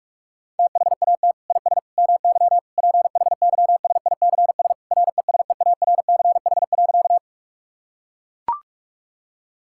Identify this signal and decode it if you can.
Morse code, heard as a keyed tone, about 41 wpm, 700 Hz. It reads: THAT IS MY PHYSICS RESEARCH.
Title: Sentences from Top 1000 Words - Review Part 5 - 45wpm